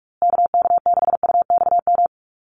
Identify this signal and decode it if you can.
Morse code, heard as a keyed tone, about 44 wpm, 700 Hz. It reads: KK6UXM
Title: USA Call Signs Repeated 2x 45wpm